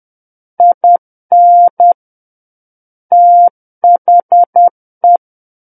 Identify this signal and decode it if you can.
Morse code, heard as a keyed tone, about 10 wpm, 700 Hz. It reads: IN THE
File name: Book One / 01-Chapter 1 - 10 WPM